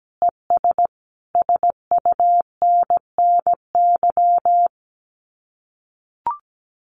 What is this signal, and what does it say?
Morse code, heard as a keyed tone, about 17 wpm, 700 Hz. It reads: ES SUNNY
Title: Common QSO Phrases Encore 17wpm